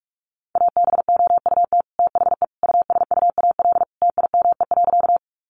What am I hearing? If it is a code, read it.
Morse code, heard as a keyed tone, about 44 wpm, 700 Hz. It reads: ABOUT THE USUAL TIME.
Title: Sentences from Top 400 Words 45wpm